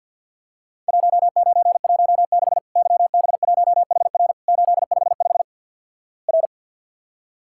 Morse code, heard as a keyed tone, about 48 wpm, 700 Hz. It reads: 1916 YB1HR 855 R
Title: Worked All Europe DX Contest - QTC Exchange with Repeats - 50wpm